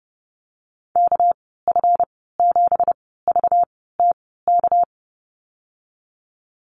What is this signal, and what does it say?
Morse code, heard as a keyed tone, about 30 wpm, spaced as for 17 wpm, 700 Hz. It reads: KF7VTK
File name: Lesson 37g - Letter x - Call Signs 30wpm